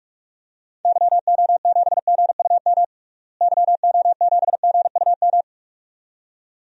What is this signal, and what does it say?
Morse code, heard as a keyed tone, about 42 wpm, 700 Hz. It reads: YO7GUM YO7GUM
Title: DXCC-100 Call Signs - 6 Characters - Repeated 2x - 45wpm